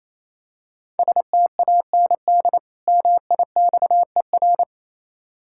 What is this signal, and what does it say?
Morse code, heard as a keyed tone, about 28 wpm, 700 Hz. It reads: STAND MIXER
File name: Easy Words - Speed-Racing with Hints - 22wpm